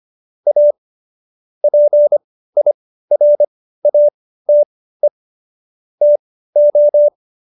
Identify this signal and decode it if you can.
Morse code, heard as a keyed tone, about 25 wpm, spaced as for 15 wpm, 590 Hz. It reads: A PIRATE TO
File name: Jokes-25-char-15-eff